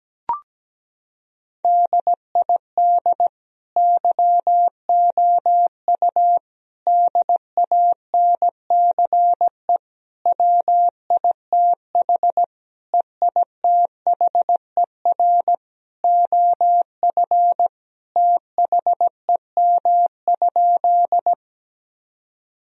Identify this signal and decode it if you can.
Morse code, heard as a keyed tone, about 17 wpm, 700 Hz. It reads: DID YOU DANCE WITH EITHER OF THEM?
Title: Sentences from Top 1000 Words - Review Part 4 - 17wpm